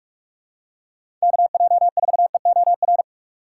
Morse code, heard as a keyed tone, about 43 wpm, 700 Hz. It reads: KJ4EOR
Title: USA Call Signs - 6 Characters - 6x Character Spacing - 45wpm